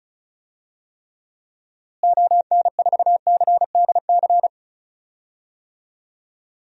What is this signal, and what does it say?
Morse code, heard as a keyed tone, about 36 wpm, 700 Hz. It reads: ON4CDC
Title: DXCC-100 Call Signs - 6 Characters - Repeated 3x - 35wpm